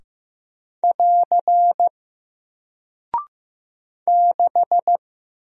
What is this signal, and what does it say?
Morse code, heard as a keyed tone, about 15 wpm, 700 Hz. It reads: <AR> 6
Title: Single Character (Letter:Number:Punctuation:Prosign) Alt 15wpm